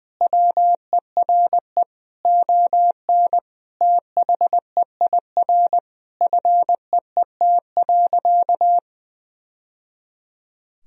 Morse code, heard as a keyed tone, about 20 wpm, 700 Hz. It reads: WERE ON THEIR FEET.
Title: Sentences from Top 1000 Words - Review Part 3 - 20wpm